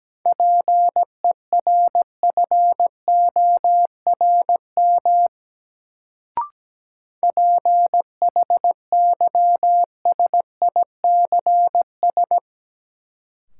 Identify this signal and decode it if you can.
Morse code, heard as a keyed tone, about 17 wpm, 700 Hz. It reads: PERFORM PHYSICS
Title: Sentences from Top 1000 Words - Review Part 5 - New Word Drill - 17wpm